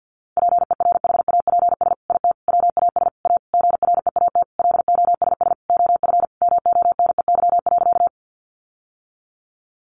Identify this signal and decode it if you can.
Morse code, heard as a keyed tone, about 50 wpm, 700 Hz. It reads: PERHAPS IT WAS A GREAT LOSS OF MONEY.
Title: Sentences from Top 1400 Words 50wpm